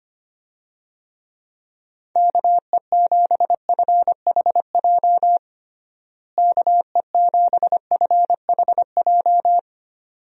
Morse code, heard as a keyed tone, about 25 wpm, 700 Hz. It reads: KE7FHJ KE7FHJ
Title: USA Call Signs - 6 Characters - Repeated 2x - 25wpm